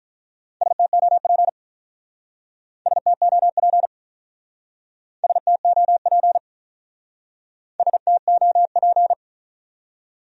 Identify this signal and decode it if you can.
Morse code, heard as a keyed tone, about 47 wpm, 700 Hz. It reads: STOP STOP STOP STOP